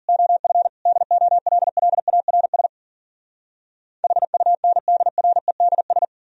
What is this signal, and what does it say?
Morse code, heard as a keyed tone, about 54 wpm, 700 Hz. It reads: OF DOLLARS HUNDREDS